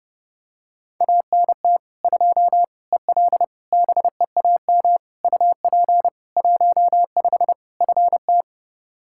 Morse code, heard as a keyed tone, about 32 wpm, 700 Hz. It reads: ANT 2 EL BEAM UP 15 FT
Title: Practice QSO - Exchange 2 - Info - 30wpm